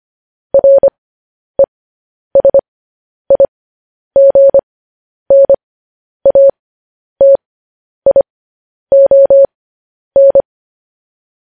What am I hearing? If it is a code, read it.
Morse code, heard as a keyed tone, about 25 wpm, spaced as for 10 wpm, 550 Hz. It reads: RESIGNATION
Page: news headlines in morse code at 10 wpm